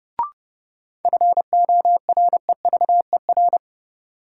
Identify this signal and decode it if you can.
Morse code, heard as a keyed tone, about 30 wpm, 700 Hz. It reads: FOREVER
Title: Sentences from Top 1600 Words - New Word Drill - 30wpm